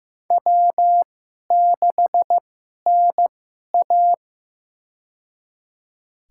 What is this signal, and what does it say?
Morse code, heard as a keyed tone, about 15 wpm, spaced as for 11 wpm, 700 Hz. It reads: W6NA
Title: USA Call Signs - 4 Characters - 2x Character Spacing - 15wpm